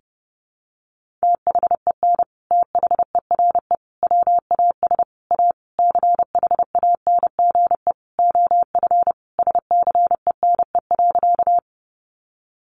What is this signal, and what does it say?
Morse code, heard as a keyed tone, about 30 wpm, 700 Hz. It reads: THEN THERE WAS A CHANGE OF SCENE.
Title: Sentences from Top 1100 Words 30wpm